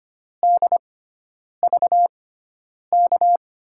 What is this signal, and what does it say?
Morse code, heard as a keyed tone, about 25 wpm, spaced as for 9 wpm, 700 Hz. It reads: DVK
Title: USA Call Signs - 6 Characters - 6x Character Spacing - 25wpm